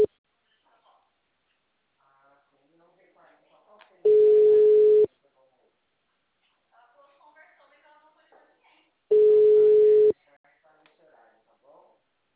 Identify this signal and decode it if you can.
Ringback tone (Europe, 425 Hz), with speech.
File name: out- -2006- - - .238